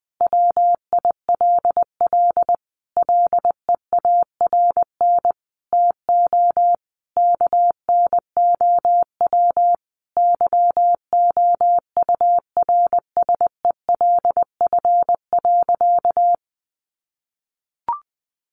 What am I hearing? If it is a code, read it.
Morse code, heard as a keyed tone, about 19 wpm, 700 Hz. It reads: WILL LEARN TO KNOW YOURSELF.
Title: Sentences from Top 600 Words 20wpm